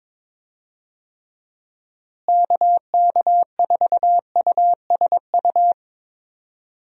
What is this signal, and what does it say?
Morse code, heard as a keyed tone, about 22 wpm, 700 Hz. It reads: KK4USU
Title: USA Call Signs - 6 Characters - Repeated 3x - 22wpm